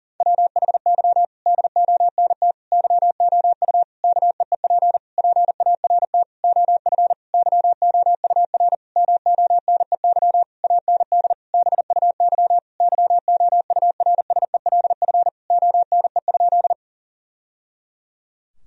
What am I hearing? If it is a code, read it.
Morse code, heard as a keyed tone, about 40 wpm, 700 Hz. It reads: WHY DONT YOU KEEP PART OF YOUR MONEY AND BUY YOURSELF ONE?
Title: Sentences from Top 1000 Words - Review Part 4 - 40wpm